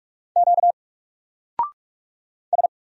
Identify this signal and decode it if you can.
Morse code, heard as a keyed tone, about 44 wpm, 700 Hz. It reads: Q S
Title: Single Letter-Number 45wpm